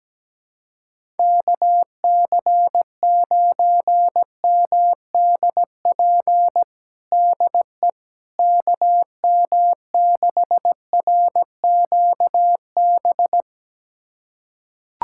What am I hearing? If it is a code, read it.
Morse code, heard as a keyed tone, about 17 wpm, 700 Hz. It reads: KC9MDP DE KM6RQB
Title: USA Callsign DE Callsign - 6 Characters - 17wpm